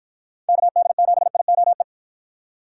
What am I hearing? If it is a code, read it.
Morse code, heard as a keyed tone, about 49 wpm, 700 Hz. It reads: KD7IOE